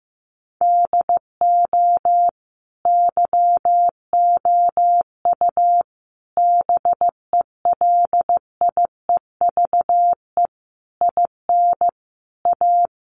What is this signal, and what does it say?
Morse code, heard as a keyed tone, about 15 wpm, 700 Hz. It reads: DO YOU BELIEVE IN A
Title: Sentences from Top 1000 Words 15wpm